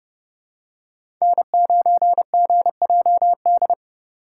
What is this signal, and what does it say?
Morse code, heard as a keyed tone, about 29 wpm, 700 Hz. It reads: N9GJD
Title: USA Call Signs - 5 Characters - 4x Character Spacing - 30wpm